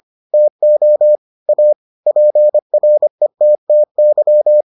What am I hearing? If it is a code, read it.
Morse code, heard as a keyed tone, about 25 wpm, 600 Hz. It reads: TO A PRETTY